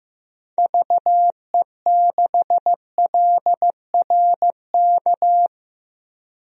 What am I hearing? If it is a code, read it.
Morse code, heard as a keyed tone, about 15 wpm, 700 Hz. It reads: VE6LRK
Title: DXCC-100 Call Signs - 6 Characters - 15wpm